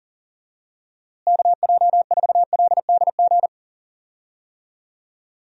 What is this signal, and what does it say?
Morse code, heard as a keyed tone, about 40 wpm, 700 Hz. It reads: KJ4LDG
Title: USA Call Signs - 6 Characters - Repeated 3x - 40wpm